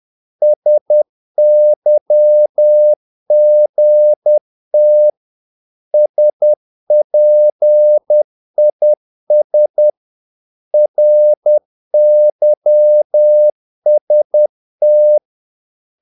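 Morse code, heard as a keyed tone, about 10 wpm, 600 Hz. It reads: SYGT SPIS RYST
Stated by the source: Morse practice of short Danish words at 10 wpm, from CW med Gnister.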